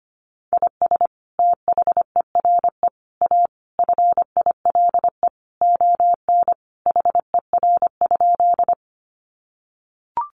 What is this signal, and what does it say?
Morse code, heard as a keyed tone, about 25 wpm, 700 Hz. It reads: IS THERE A FILE ON HER?
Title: Sentences from Top 1200 Words 25wpm